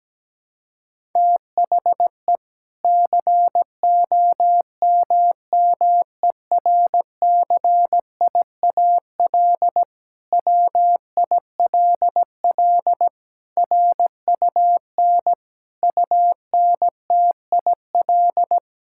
Morse code, heard as a keyed tone, about 17 wpm, 700 Hz. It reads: THE COMMERCIAL WILL RUN UNTIL
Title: Sentences from Top 1000 Words - Review Part 5 - 17wpm